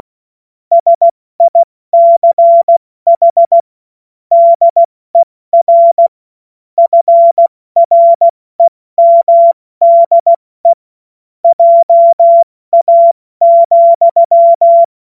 Morse code, heard as a keyed tone, about 16 wpm, 680 Hz. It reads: SICH DER FREMDE JA,
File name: Morsekurs / Witze / Witz 21. Februar 2023 70 F80